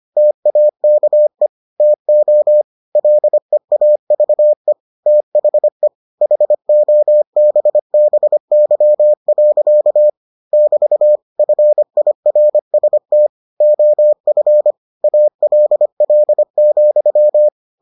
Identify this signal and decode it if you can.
Morse code, heard as a keyed tone, about 25 wpm, 600 Hz. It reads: TAKE TO LEAVE THE HOBBY. <BT> FIRST OF ALL,